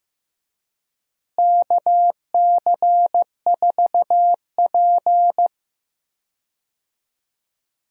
Morse code, heard as a keyed tone, about 15 wpm, 700 Hz. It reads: KC4P
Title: USA Call Signs - 4 Characters - Repeated 2x - 15wpm